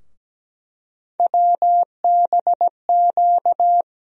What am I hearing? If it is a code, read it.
Morse code, heard as a keyed tone, about 17 wpm, 700 Hz. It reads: WBQ
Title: USA Call Sign Postfixes - 3 Letters 17wpm